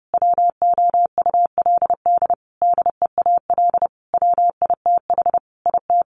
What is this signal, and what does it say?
Morse code, heard as a keyed tone, about 30 wpm, 700 Hz. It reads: WOULD DEAL WITH IT
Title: Sentences from Top 1500 Words 30wpm